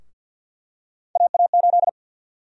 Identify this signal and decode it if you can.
Morse code, heard as a keyed tone, about 48 wpm, 700 Hz. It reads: AA8
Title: USA Call Sign Prefixes - 50wpm